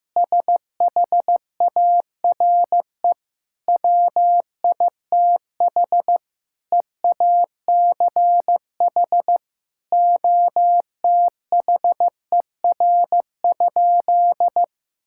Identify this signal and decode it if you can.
Morse code, heard as a keyed tone, about 16 wpm, 700 Hz. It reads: SHARE WITH EACH OTHER?
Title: Sentences from Top 1000 Words - Review Part 6 - 15wpm